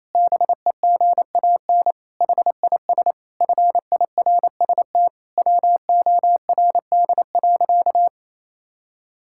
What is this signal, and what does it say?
Morse code, heard as a keyed tone, about 28 wpm, 700 Hz. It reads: BEGAN HIS FIRST WORD.
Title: Pangram Challenge v2 - 28wpm